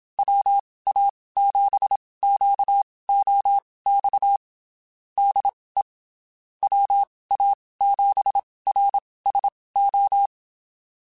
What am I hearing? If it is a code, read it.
Morse code, heard as a keyed tone, about 27 wpm, spaced as for 19 wpm, 800 Hz. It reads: WA7QOX DE WA7RSO
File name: (spaced) / QSOs Spaced-mp3s / WA7QOX 1 @25 sp